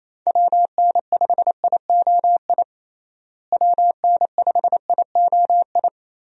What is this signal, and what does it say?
Morse code, heard as a keyed tone, about 28 wpm, 700 Hz. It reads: WN5IOI WN5IOI
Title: USA Call Signs - 6 Characters - Repeated 2x - 28wpm